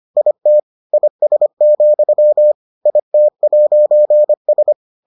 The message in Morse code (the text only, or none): IT IS, IT'S